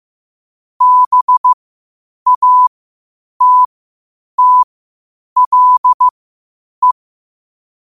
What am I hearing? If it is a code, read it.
Morse code, heard as a keyed tone, about 15 wpm, spaced as for 8 wpm, 1000 Hz. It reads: BATTLE
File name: 8wpm